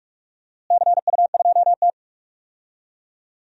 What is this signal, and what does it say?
Morse code, heard as a keyed tone, about 47 wpm, 700 Hz. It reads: CU2T
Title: DXCC-100 Call Signs - 4 Characters - Repeated 3x - 45wpm